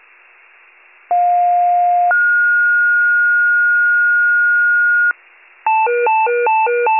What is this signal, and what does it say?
Two-tone page, heard as a one-second tone, then a three-second tone, about 710 then 1430 Hz. It is heard from 1.1 to 5.1 s.